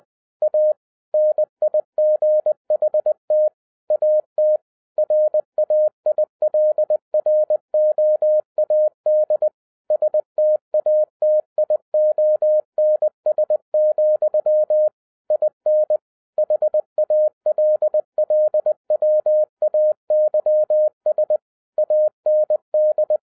Morse code, heard as a keyed tone, about 20 wpm, 610 Hz. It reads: A NIGHT AT RAILROAD STATIONS, IN HALLWAYS AND
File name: 2011-07-mp3 / n33 THE 3.11 2 20wpm